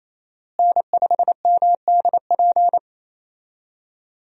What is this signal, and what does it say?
Morse code, heard as a keyed tone, about 27 wpm, 700 Hz. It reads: N5MDP
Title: USA Call Signs - 5 Characters - Repeated 3x - 28wpm